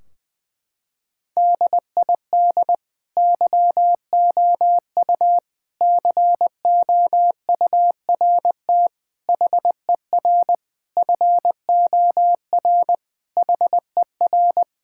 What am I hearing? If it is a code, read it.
Morse code, heard as a keyed tone, about 20 wpm, 700 Hz. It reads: DID YOU COURT HER FOR HER
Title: Sentences from Top 1600 Words 20wpm